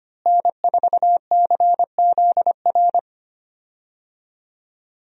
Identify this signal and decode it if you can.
Morse code, heard as a keyed tone, about 25 wpm, 700 Hz. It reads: N4CZR
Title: USA Call Signs - 5 Characters - Repeated 2x - 25wpm